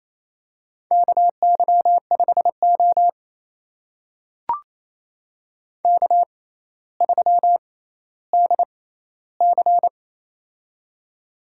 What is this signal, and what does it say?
Morse code, heard as a keyed tone, about 28 wpm, spaced as for 14 wpm, 700 Hz. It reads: KY5O K3DC
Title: USA Call Signs - 4 Characters - 6x Character Spacing - 28wpm